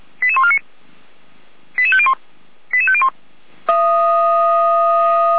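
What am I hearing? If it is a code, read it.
Five-tone selective call (ZVEI), heard as quick runs of five tones, tones of about 70 ms. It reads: 88149 88501 88501
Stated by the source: ZVEI five-tone selective call 88140 received by SDR, recorded on a feuerwehr-orlamuende.org fire brigade server.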